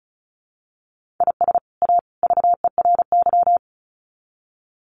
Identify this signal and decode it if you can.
Morse code, heard as a keyed tone, about 34 wpm, 700 Hz. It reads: IS A VERY
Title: Sets of 3 Words 35wpm